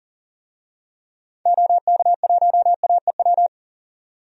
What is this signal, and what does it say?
Morse code, heard as a keyed tone, about 40 wpm, 700 Hz. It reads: OK1AEW
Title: DXCC-100 Call Signs - 6 Characters - 40wpm